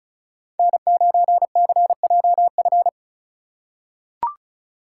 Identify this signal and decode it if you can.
Morse code, heard as a keyed tone, about 34 wpm, 690 Hz. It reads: N9CJF
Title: USA Call Signs - 5 Characters - 3x Character Spacing - 35wpm